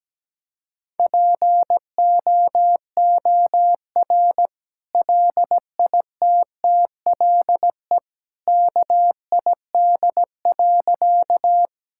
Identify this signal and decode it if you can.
Morse code, heard as a keyed tone, about 17 wpm, 700 Hz. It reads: POOR LITTLE KID.
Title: Sentences from Top 1600 Words 17wpm